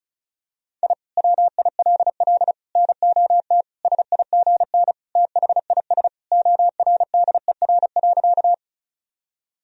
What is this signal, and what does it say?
Morse code, heard as a keyed tone, about 36 wpm, 700 Hz. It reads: I WILL NOT SIGN THIS ORDER.
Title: Sentences from Top 1000 Words - Review Part 2 - 35wpm